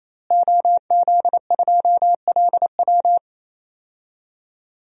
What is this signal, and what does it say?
Morse code, heard as a keyed tone, about 27 wpm, 700 Hz. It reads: OZ2LW
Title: DXCC-100 Call Signs - 5 Characters - Repeated 3x - 28wpm